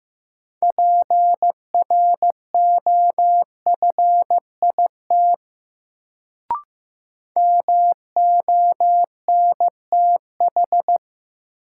Morse code, heard as a keyed tone, about 15 wpm, 700 Hz. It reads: PROFIT MONTH
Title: Sentences from Top 800 Words - New Word Drill - 15wpm